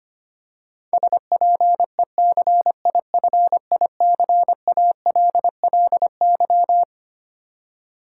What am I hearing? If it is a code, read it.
Morse code, heard as a keyed tone, about 25 wpm, 700 Hz. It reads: SPECIFICALLY
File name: Twelve Letter Words - 25wpm